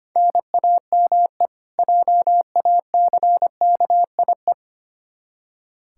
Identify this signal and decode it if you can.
Morse code, heard as a keyed tone, about 25 wpm, 700 Hz. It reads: NAME JACKIE
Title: Practice QSO - Exchange 1 - Beginning - 25wpm